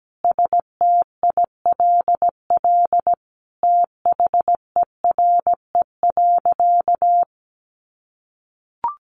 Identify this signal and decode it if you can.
Morse code, heard as a keyed tone, about 17 wpm, 700 Hz. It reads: STILL THERE.
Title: A Sound Of Thunder 17wpm